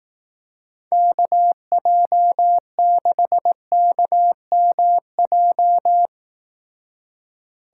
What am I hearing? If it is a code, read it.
Morse code, heard as a keyed tone, about 18 wpm, 700 Hz. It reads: KJ6KMJ
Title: Speed Racing - USA 2x3 Call Signs 15wpm